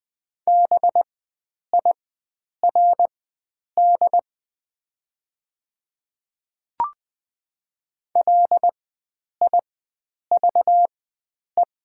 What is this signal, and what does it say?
Morse code, heard as a keyed tone, about 20 wpm, spaced as for 9 wpm, 700 Hz. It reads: BIRD LIVE
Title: 4x Character Spacing - Top 500 Words - 20wpm